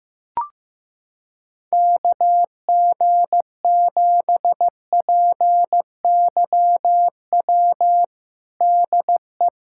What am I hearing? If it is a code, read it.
Morse code, heard as a keyed tone, about 15 wpm, 700 Hz. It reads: KG7PYW DE
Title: USA Callsign DE Callsign - 6 Characters - 15wpm